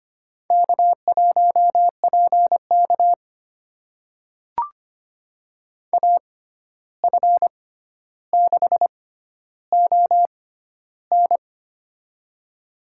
Morse code, heard as a keyed tone, about 25 wpm, spaced as for 12 wpm, 700 Hz. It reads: K1PK AF6ON
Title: USA Call Signs - 6x Character Spacing - 25wpm